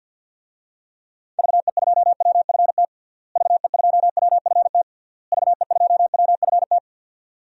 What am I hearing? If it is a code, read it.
Morse code, heard as a keyed tone, about 52 wpm, 700 Hz. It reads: VE2WFT VE2WFT VE2WFT
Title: DXCC-100 Call Signs - 6 Characters - Repeated 3x - 50wpm